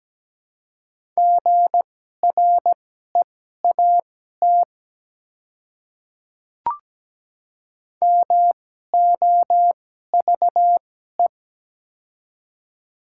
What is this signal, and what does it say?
Morse code, heard as a keyed tone, about 17 wpm, spaced as for 12 wpm, 700 Hz. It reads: GREAT MOVE
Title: 2x Character Spacing - Top 200 Words - 17wpm